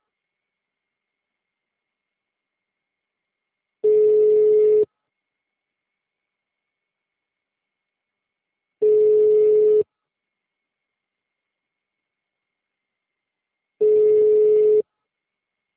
Ringback tone (Europe, 425 Hz).